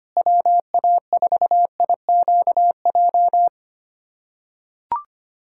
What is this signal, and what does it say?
Morse code, heard as a keyed tone, about 25 wpm, 700 Hz. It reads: WA4IQJ
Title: USA Callsign DE Callsign - 6 Characters - 25wpm